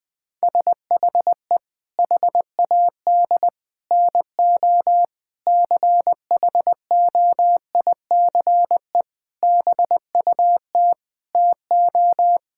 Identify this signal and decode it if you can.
Morse code, heard as a keyed tone, about 20 wpm, 700 Hz. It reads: SHE HAD NO CHOICE BUT TO
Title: Sentences with 5-Word Phrases - 20wpm